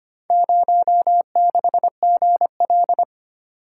Morse code, heard as a keyed tone, about 25 wpm, 700 Hz. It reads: 06GL
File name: practice / 25wpm / Four Letter-Number Combinations - 25wpm